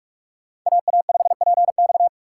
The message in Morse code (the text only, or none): AA5PX